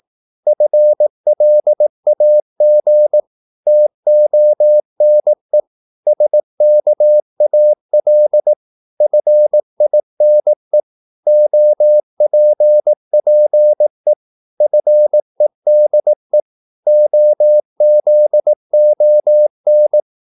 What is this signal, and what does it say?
Morse code, heard as a keyed tone, about 18 wpm, 600 Hz. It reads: FLAG TONE SKAL FINE OPPE FEDE OZON
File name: Korte ord DK 18wpm